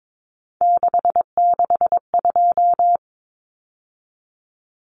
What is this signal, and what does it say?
Morse code, heard as a keyed tone, about 22 wpm, 700 Hz. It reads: 662
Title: Numbers - 3-Digit 22wpm